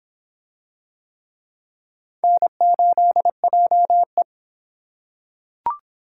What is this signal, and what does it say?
Morse code, heard as a keyed tone, about 23 wpm, 700 Hz. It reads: N8JE